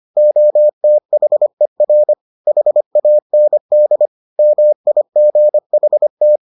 Morse code, heard as a keyed tone, about 25 wpm, 600 Hz. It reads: OTHER HAND MIGHT